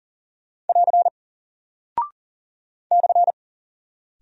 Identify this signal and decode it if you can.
Morse code, heard as a keyed tone, about 40 wpm, 700 Hz. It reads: <AR> /
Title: Single Character (Letter:Number:Punctuation:Prosign) Alt 40wpm